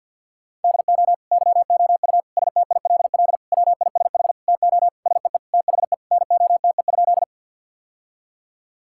Morse code, heard as a keyed tone, about 50 wpm, 700 Hz. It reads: DO YOU STILL WISH TO SEE THE NOTE?
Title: Sentences from Top 1600 Words 50wpm